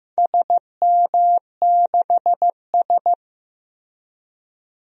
Morse code, heard as a keyed tone, about 15 wpm, 700 Hz. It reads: SM6S
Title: DXCC-100 Call Signs - 4 Characters - Repeated 2x - 15wpm